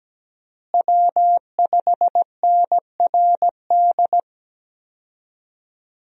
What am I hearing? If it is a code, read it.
Morse code, heard as a keyed tone, about 17 wpm, 700 Hz. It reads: W5NRD
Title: USA Call Signs - 5 Characters - Repeated 2x - 17wpm